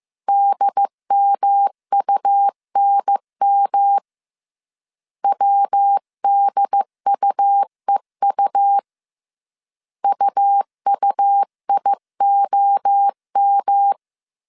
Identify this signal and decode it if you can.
Morse code, heard as a keyed tone, about 15 wpm, 800 Hz. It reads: DMUNM WDUEU UUIOM
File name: CW / DET / 1 3 1500C S A 14wpm 2447